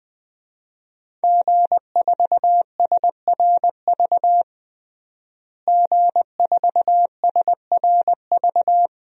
Morse code, heard as a keyed tone, about 20 wpm, 700 Hz. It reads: G4SRV G4SRV
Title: DXCC-100 Call Signs - 5 Characters - Repeated 3x - 20wpm